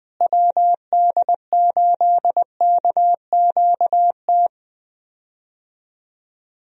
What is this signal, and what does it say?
Morse code, heard as a keyed tone, about 20 wpm, 700 Hz. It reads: WD8KQT
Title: USA Call Signs - 6 Characters - Repeated 3x - 20wpm